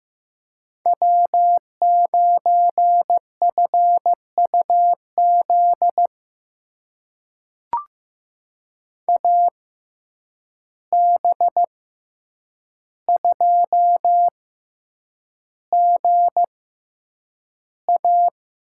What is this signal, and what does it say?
Morse code, heard as a keyed tone, about 15 wpm, spaced as for 8 wpm, 700 Hz. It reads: W9FUZ AB2GA
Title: USA Call Signs - 5 Characters - 6x Character Spacing - 15wpm